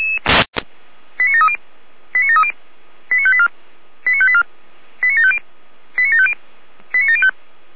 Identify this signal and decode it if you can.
Five-tone selective call (ZVEI), heard as quick runs of five tones, tones of about 70 ms. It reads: 89730 89730 89584 89584 89750 89750 89705